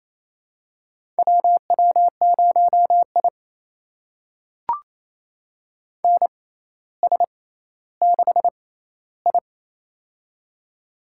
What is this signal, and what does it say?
Morse code, heard as a keyed tone, about 28 wpm, spaced as for 14 wpm, 700 Hz. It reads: WW0I NS6I
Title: USA Call Signs - 6x Character Spacing - 28wpm